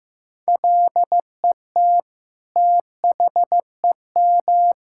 Morse code, heard as a keyed tone, about 15 wpm, 700 Hz. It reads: LET THEM
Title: Sentences from Top 1000 Words - Review Part 6 - 15wpm